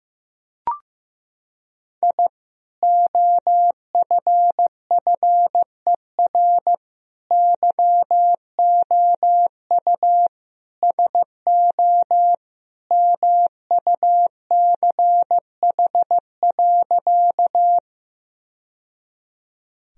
Morse code, heard as a keyed tone, about 15 wpm, 700 Hz. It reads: I OFFER YOU SO MUCH.
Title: Sentences from Top 1000 Words - Review Part 4 - 15wpm